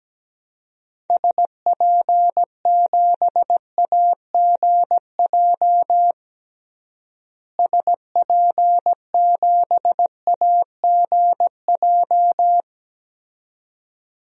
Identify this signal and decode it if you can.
Morse code, heard as a keyed tone, about 17 wpm, 700 Hz. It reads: SP7AGJ SP7AGJ
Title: DXCC-100 Call Signs - 6 Characters - Repeated 2x - 17wpm